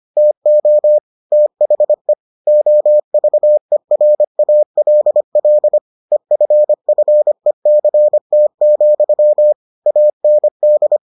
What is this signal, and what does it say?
Morse code, heard as a keyed tone, about 25 wpm, 600 Hz. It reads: TO THE OVERALL EFFECT, AND